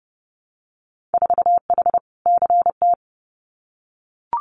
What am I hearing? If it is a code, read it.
Morse code, heard as a keyed tone, about 30 wpm, 700 Hz. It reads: 4H CT
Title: Winter Field Day Exchanges Encore 30wpm